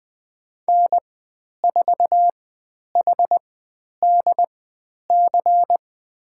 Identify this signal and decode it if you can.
Morse code, heard as a keyed tone, about 20 wpm, spaced as for 10 wpm, 700 Hz. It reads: N4HDC
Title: 095 Lesson 16g - Letter c - Call Signs 20wpm